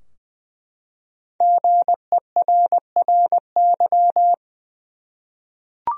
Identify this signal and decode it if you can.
Morse code, heard as a keyed tone, about 20 wpm, 700 Hz. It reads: GERRY
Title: USA Common Names - 20wpm